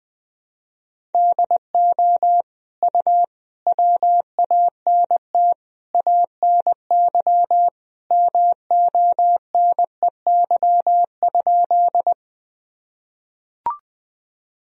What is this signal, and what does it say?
Morse code, heard as a keyed tone, about 20 wpm, 700 Hz. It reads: DO U WANT ANY MONEY?
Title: Sentences from Top 400 Words and Abbreviations 20wpm